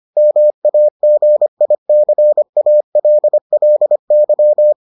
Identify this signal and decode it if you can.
Morse code, heard as a keyed tone, about 25 wpm, 600 Hz. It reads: MAGICALLY